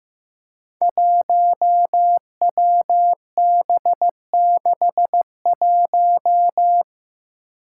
Morse code, heard as a keyed tone, about 15 wpm, 700 Hz. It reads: 1WB61
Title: Five Challenging Letter-Number Combinations 15wpm